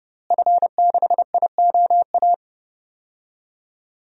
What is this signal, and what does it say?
Morse code, heard as a keyed tone, about 30 wpm, 700 Hz. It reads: F6IOA
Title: DXCC-100 Call Signs - 5 Characters - Repeated 3x - 30wpm